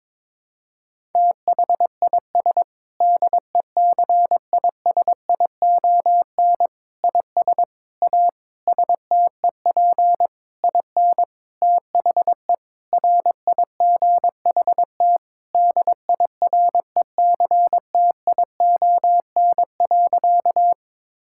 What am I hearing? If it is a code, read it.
Morse code, heard as a keyed tone, about 22 wpm, 700 Hz. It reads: THIS DECISION IS A STEP IN THE RIGHT DIRECTION.
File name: Sentences with 7-Word Phrases - 22wpm